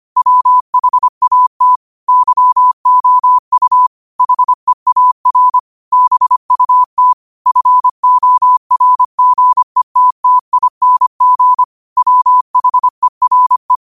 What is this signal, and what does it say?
Morse code, heard as a keyed tone, about 25 wpm, 1000 Hz. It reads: WHAT YOU HEAR BUT FORGETTING WHERE
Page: New quotes every day in morse code at 25 Words per minute.